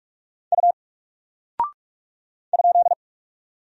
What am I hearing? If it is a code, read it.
Morse code, heard as a keyed tone, about 44 wpm, 700 Hz. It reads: U ?
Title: Single Character (Letter:Number:Punctuation:Prosign) 45wpm